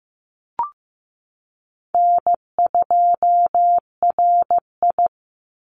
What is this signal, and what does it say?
Morse code, heard as a keyed tone, about 15 wpm, 700 Hz. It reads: N2RI
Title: Easy Call Signs Repeated 2x - 15wpm